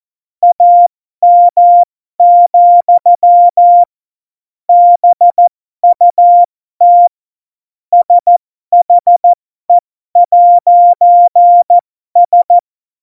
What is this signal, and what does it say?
Morse code, heard as a keyed tone, about 14 wpm, spaced as for 12 wpm, 700 Hz. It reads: AM, BUT SHE'S
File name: Morsekurs / Witze